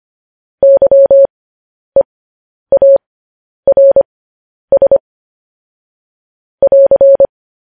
Morse code, heard as a keyed tone, about 25 wpm, spaced as for 10 wpm, 550 Hz. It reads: YEARS <AR>
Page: news headlines in morse code at 10 wpm